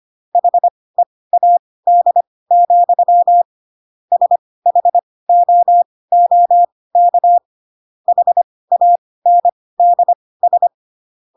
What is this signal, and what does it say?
Morse code, heard as a keyed tone, about 25 wpm, spaced as for 18 wpm, 700 Hz. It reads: HEAD, SHOOK HANDS
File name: Pinocchio / Pinocchio ch2 12 25 ik1hge